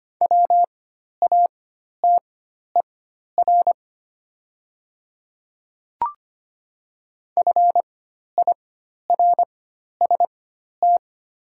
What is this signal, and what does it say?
Morse code, heard as a keyed tone, about 25 wpm, spaced as for 12 wpm, 700 Hz. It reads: WATER FIRST
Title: 4x Character Spacing - Top 100 Words - 25wpm